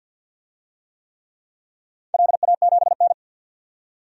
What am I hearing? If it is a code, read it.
Morse code, heard as a keyed tone, about 46 wpm, 700 Hz. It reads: LA7N
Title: DXCC-100 Call Signs - 4 Characters - 50wpm